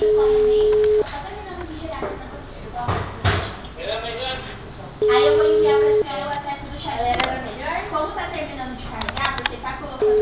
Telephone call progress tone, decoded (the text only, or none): ringback tone